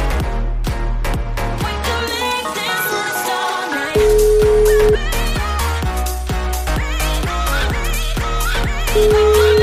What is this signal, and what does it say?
Ringback tone (Europe, 425 Hz), with noise.